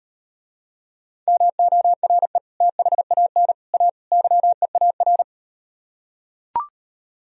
Morse code, heard as a keyed tone, about 38 wpm, 700 Hz. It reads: MORE THAN A YEAR